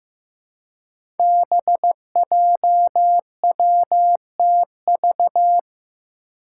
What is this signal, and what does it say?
Morse code, heard as a keyed tone, about 15 wpm, 700 Hz. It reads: BJWTV
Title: Five Challenging Letter-Number Combinations 15wpm